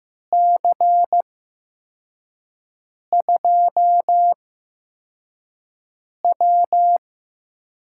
Morse code, heard as a keyed tone, about 15 wpm, spaced as for 4 wpm, 700 Hz. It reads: C2W
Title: USA Call Signs - 6 Characters - 8x Character Spacing - 15wpm